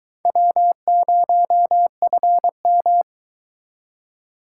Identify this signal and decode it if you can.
Morse code, heard as a keyed tone, about 22 wpm, 700 Hz. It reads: W0FM